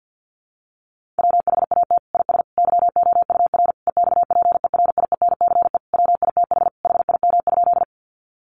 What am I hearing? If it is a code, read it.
Morse code, heard as a keyed tone, about 50 wpm, 700 Hz. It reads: WHAT IS YOUR EXPERIENCE WITH HIM?
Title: Sentences from Top 600 Words 50wpm